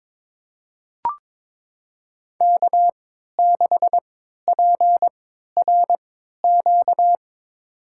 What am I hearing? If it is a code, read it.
Morse code, heard as a keyed tone, about 22 wpm, spaced as for 13 wpm, 700 Hz. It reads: K6PRQ
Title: USA Call Signs - 5 Characters - 3x Character Spacing - 22wpm